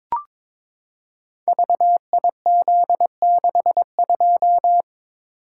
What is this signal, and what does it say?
Morse code, heard as a keyed tone, about 22 wpm, 700 Hz. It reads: VIZ62
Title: Code Groups - 5 Letter:Numbers - 22wpm